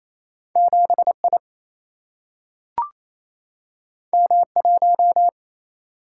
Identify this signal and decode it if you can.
Morse code, heard as a keyed tone, about 28 wpm, 700 Hz. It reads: 7I M1
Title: Single-Letter Single-Number 28wpm